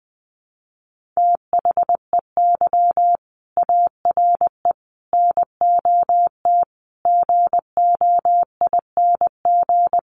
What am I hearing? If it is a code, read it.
Morse code, heard as a keyed tone, about 20 wpm, 700 Hz. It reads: THEY ARE NOT GOING
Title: Sets of 5 Words Encore2 20wpm